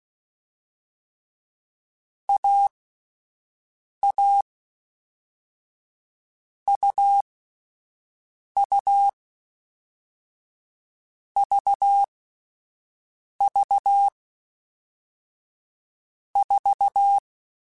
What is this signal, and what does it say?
Morse code, heard as a keyed tone, about 16 wpm, 800 Hz. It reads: A A U U V V 4
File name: Difficult Combos / Difficult Combos-mp3s / AUV AUV4 @10